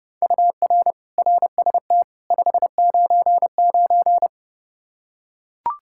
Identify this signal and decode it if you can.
Morse code, heard as a keyed tone, about 30 wpm, 700 Hz. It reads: UR RST 599
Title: K1USN SST QSO Practice 30wpm